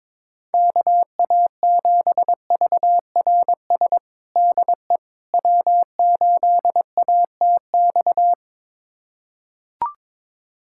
Morse code, heard as a keyed tone, about 22 wpm, 700 Hz. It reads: KA7VRS DE W8ATX
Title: USA Callsign DE Callsign 22wpm